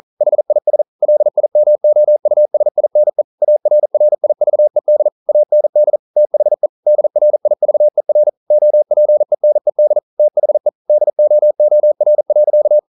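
Morse code, heard as a keyed tone, about 41 wpm, 600 Hz. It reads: HIS LIMOUSINE ARRIVED AND THE DRIVER OPENED THE DOOR.
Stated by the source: Morse code practice at 41 wpm, from CW med Gnister.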